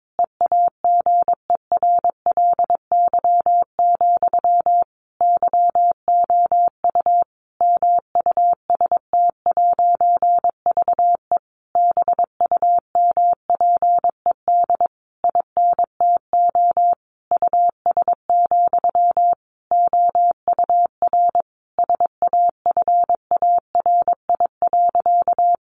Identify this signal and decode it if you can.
Morse code, heard as a keyed tone, about 22 wpm, 700 Hz. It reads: EAGERLY, YOU MUST'VE BUMPED INTO US, OUR SAFARI.
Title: A Sound Of Thunder 22wpm